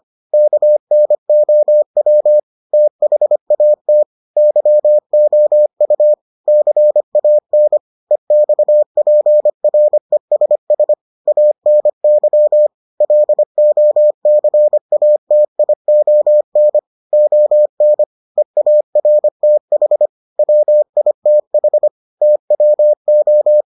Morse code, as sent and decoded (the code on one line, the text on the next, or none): -.- -. --- .-- / - .... .- - / -.-- --- ..- / -.-. .- -. / . -..- .--. .-. ... ... / .- -. -.-- / .-.. --- -.-. .- - .. --- -. / --- -. / . .- .-. - .... / .-- .. - .... / - .-- ---
KNOW THAT YOU CAN EXPRESS ANY LOCATION ON EARTH WITH TWO